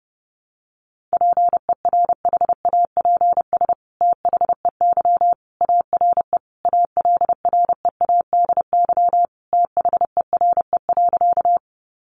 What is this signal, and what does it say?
Morse code, heard as a keyed tone, about 30 wpm, 700 Hz. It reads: PERHAPS THEY ARE ALREADY THERE.
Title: Sentences from Top 700 Words 30wpm